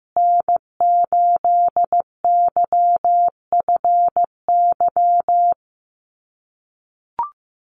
Morse code, heard as a keyed tone, about 15 wpm, 700 Hz. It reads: N8YFY
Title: USA Call Signs Repeated 2x 15wpm